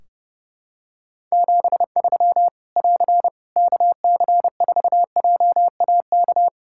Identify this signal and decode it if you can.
Morse code, heard as a keyed tone, about 32 wpm, 700 Hz. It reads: 73 <AR> KC4JAK
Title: Practice QSO - Exchange 4 - Ending - 30wpm